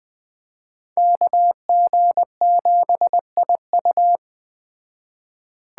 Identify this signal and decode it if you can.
Morse code, heard as a keyed tone, about 20 wpm, 700 Hz. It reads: KG7IU